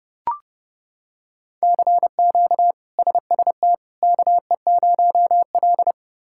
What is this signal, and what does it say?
Morse code, heard as a keyed tone, about 29 wpm, 700 Hz. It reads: CQ SST KE0L
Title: K1USN SST QSO Practice - 30wpm